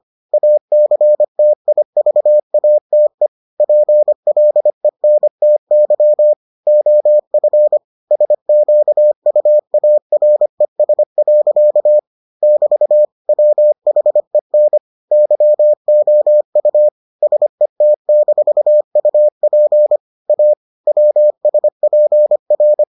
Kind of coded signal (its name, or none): Morse code